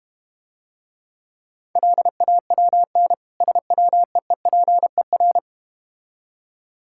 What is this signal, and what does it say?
Morse code, heard as a keyed tone, about 31 wpm, 700 Hz. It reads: LAWN SWEEPER
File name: practice / 25wpm / Difficult Words - Speed-Racing with Hints - 25wpm